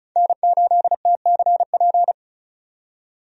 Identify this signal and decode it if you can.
Morse code, heard as a keyed tone, about 36 wpm, 700 Hz. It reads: N8TCP
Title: USA Call Signs - 5 Characters - 35wpm